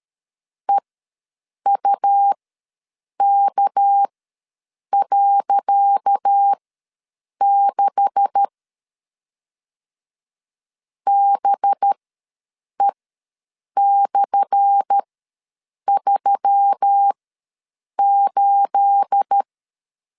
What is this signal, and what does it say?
Morse code, heard as a keyed tone, about 13 wpm, spaced as for 7 wpm, 800 Hz. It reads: EUK.6 BE/38
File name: LEZIONI / 10 SIGNS / 10C DOT ws 12wpm 3548